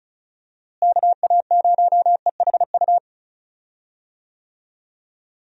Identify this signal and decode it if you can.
Morse code, heard as a keyed tone, about 36 wpm, 700 Hz. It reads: KA0EHU
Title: USA Call Signs - 6 Characters - Repeated 3x - 35wpm